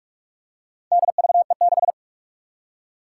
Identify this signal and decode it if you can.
Morse code, heard as a keyed tone, about 44 wpm, 700 Hz. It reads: DVE6
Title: Four Challenging Letter-Number Combinations 45wpm